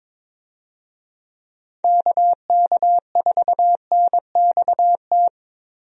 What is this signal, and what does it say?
Morse code, heard as a keyed tone, about 22 wpm, 700 Hz. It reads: KK4NXT
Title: USA Call Signs - 6 Characters - Repeated 2x - 22wpm